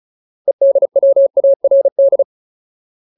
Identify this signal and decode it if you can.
Morse code, heard as a keyed tone, about 35 wpm, 550 Hz. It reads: EDWARD
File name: Ultimate Word List / 35WPM Ultimate